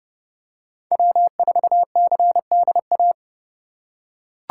Morse code, heard as a keyed tone, about 27 wpm, 700 Hz. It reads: W4CDA